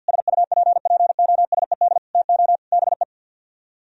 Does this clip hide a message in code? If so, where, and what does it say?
Yes: Morse code from 0.1 to 3.0 s, heard as a keyed tone, about 63 wpm, 700 Hz. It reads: SUPPOSED TO BE